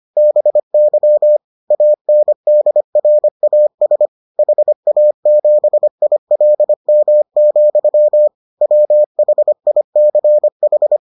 Morse code, heard as a keyed tone, about 25 wpm, 600 Hz. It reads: BY ANDRAS HA7ILM, WHICH